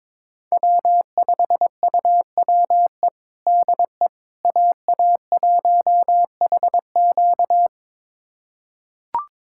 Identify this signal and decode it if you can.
Morse code, heard as a keyed tone, about 22 wpm, 700 Hz. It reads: W5UWE DE AA1HQ
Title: USA Callsign DE Callsign - 5 Characters - 22wpm